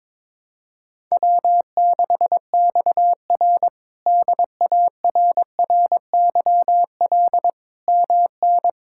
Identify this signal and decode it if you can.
Morse code, heard as a keyed tone, about 22 wpm, 700 Hz. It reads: W6XR DARRYL MN
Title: K1USN SST QSO Practice - 22wpm